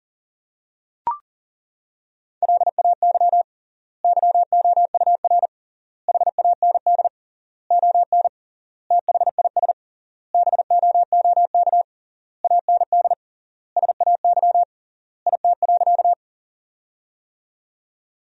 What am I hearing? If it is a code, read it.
Morse code, heard as a keyed tone, about 40 wpm, 700 Hz. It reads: LAY YOUR HAND ON THIS BOOK AND SAY IT.
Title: 3x Word-Spacing - Sentences from Top 400 Words - 40wpm